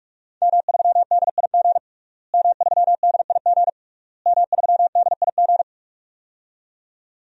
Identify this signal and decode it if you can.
Morse code, heard as a keyed tone, about 46 wpm, 700 Hz. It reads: M3DIG M3DIG M3DIG
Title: DXCC-100 Call Signs - 5 Characters - Repeated 3x - 45wpm